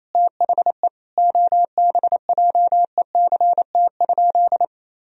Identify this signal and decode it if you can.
Morse code, heard as a keyed tone, about 28 wpm, 700 Hz. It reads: THE OBJECT?
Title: Sentences from Top 1000 Words - Review Part 3 - 28wpm